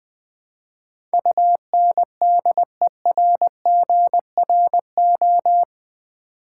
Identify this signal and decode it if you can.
Morse code, heard as a keyed tone, about 20 wpm, 700 Hz. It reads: UNDERGRO
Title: Build A Word - 4 Letters At A Time - 11-Letter Words - 20wpm